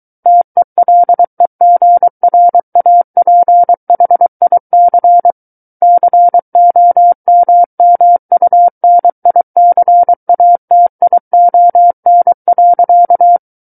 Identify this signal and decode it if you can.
Morse code, heard as a keyed tone, about 23 wpm, 700 Hz. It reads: TELEGRAPHIC COMMUNICATION.
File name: Book One / 14-Chapter 14 - 23 WPM